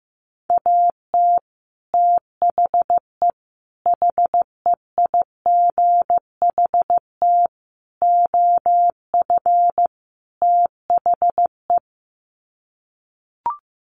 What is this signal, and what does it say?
Morse code, heard as a keyed tone, about 15 wpm, 700 Hz. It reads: AT THE HEIGHT OF THE
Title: Sets of 5 Words 15wpm